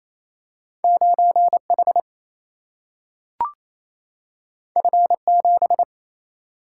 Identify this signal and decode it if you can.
Morse code, heard as a keyed tone, about 28 wpm, 700 Hz. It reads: 9 H F 7
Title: Single-Letter Single-Number 28wpm